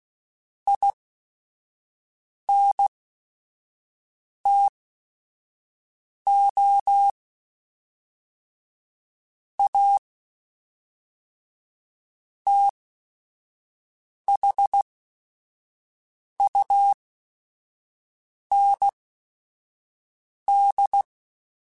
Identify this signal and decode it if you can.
Morse code, heard as a keyed tone, about 16 wpm, spaced as for 5 wpm, 800 Hz. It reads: INTO A THUND
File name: QSOs / QSOs-mp3s / WD0BGP 1@5